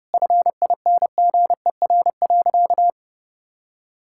Morse code, heard as a keyed tone, about 30 wpm, 700 Hz. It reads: FINGER.
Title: Sentences from Top 1600 Words 30wpm